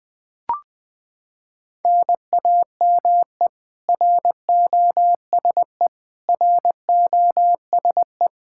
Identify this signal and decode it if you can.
Morse code, heard as a keyed tone, about 21 wpm, 700 Hz. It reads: NAME ROSE ROSE
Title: Practice QSO - Exchange 1 - Beginning - 20wpm